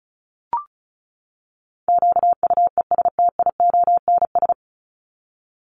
Morse code, heard as a keyed tone, about 34 wpm, 700 Hz. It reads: QUESTIONS
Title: ES:S Suffix - Top 100 Words with Base Words 35wpm